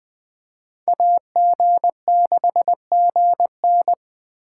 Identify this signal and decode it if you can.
Morse code, heard as a keyed tone, about 20 wpm, 700 Hz. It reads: AG6GN
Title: Speed Racing - USA 2x2 Call Signs 17wpm